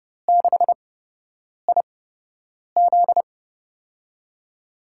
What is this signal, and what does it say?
Morse code, heard as a keyed tone, about 32 wpm, spaced as for 8 wpm, 700 Hz. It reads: 6IZ
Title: USA Call Signs - 4 Characters - 8x Character Spacing - 30wpm